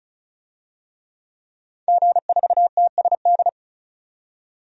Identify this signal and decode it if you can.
Morse code, heard as a keyed tone, about 34 wpm, 700 Hz. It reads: G4TSD
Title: DXCC-100 Call Signs - 5 Characters - 35wpm